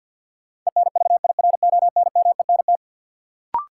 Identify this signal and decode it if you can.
Morse code, heard as a keyed tone, about 52 wpm, 700 Hz. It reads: ENVIRONMENT
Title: Sentences from Top 1000 Words - Review Part 5 - New Word Drill - 50wpm